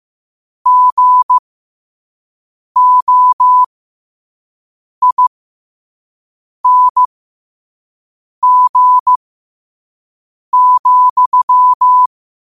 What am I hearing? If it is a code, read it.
Morse code, heard as a keyed tone, about 15 wpm, spaced as for 5 wpm, 1000 Hz. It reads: GOING,